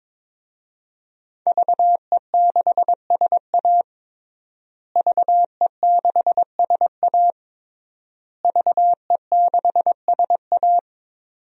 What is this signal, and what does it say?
Morse code, heard as a keyed tone, about 22 wpm, 700 Hz. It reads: VE6SA VE6SA VE6SA
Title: DXCC-100 Call Signs - 5 Characters - Repeated 3x - 22wpm